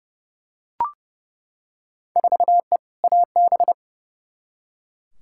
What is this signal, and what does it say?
Morse code, heard as a keyed tone, about 30 wpm, 700 Hz. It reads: 4E AB
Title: Field Day Exchanges Encore - 30wpm